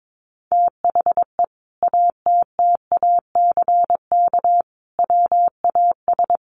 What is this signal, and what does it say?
Morse code, heard as a keyed tone, about 22 wpm, 700 Hz. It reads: THE ATTACK WAS
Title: Sentences from Top 1500 Words 22wpm